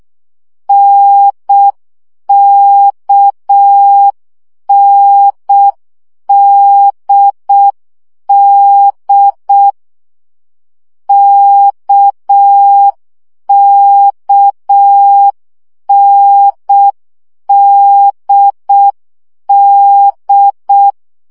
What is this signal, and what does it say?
Morse code, heard as a keyed tone, about 6 wpm, 800 Hz. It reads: NKNDD KKNDD
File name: CW gyakorlas n d k 5 baud